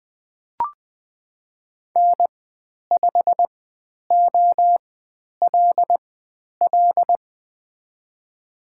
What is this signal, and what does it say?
Morse code, heard as a keyed tone, about 20 wpm, spaced as for 10 wpm, 700 Hz. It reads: N5OLL
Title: 095 Lesson 16g - Letter c - Call Signs 20wpm